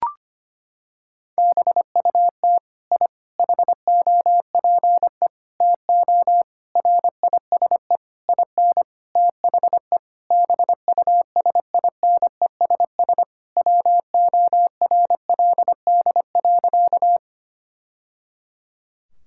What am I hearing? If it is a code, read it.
Morse code, heard as a keyed tone, about 25 wpm, 700 Hz. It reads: BUT I HOPE TO RISE IN THE BUSINESS WORLD.
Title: Sentences from Top 1000 Words - Review Part 6 - 25wpm